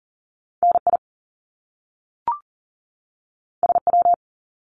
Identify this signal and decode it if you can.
Morse code, heard as a keyed tone, about 40 wpm, 700 Hz. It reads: NI SW
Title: Top Two Letter Combinations in Words 40wpm